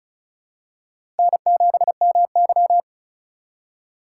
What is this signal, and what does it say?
Morse code, heard as a keyed tone, about 34 wpm, 700 Hz. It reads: N7MY
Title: USA Call Signs - 4x Character Spacing - 35wpm